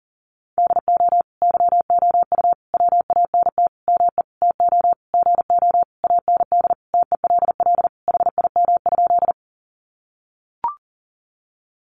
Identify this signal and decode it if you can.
Morse code, heard as a keyed tone, about 40 wpm, 700 Hz. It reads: DO YOU WANT ME TO GO AND TELL HIM?
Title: Sentences from Top 200 Words 40wpm